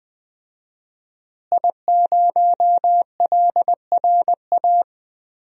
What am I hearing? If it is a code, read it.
Morse code, heard as a keyed tone, about 20 wpm, 700 Hz. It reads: I0LRA
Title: DXCC-100 Call Signs - 5 Characters - Repeated 3x - 20wpm